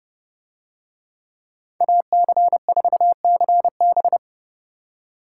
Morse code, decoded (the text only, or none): AC4CB